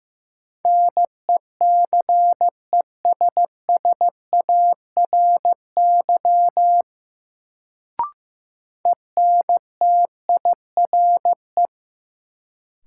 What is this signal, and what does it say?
Morse code, heard as a keyed tone, about 15 wpm, 700 Hz. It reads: NECESSARY ENTIRE
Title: Sentences from Top 800 Words - New Word Drill - 15wpm